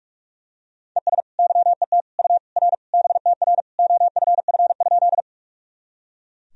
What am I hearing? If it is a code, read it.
Morse code, heard as a keyed tone, about 43 wpm, 700 Hz. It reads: ES YET U R BTR OFF?
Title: Sentences from Top 500 Words and Abbreviations 45wpm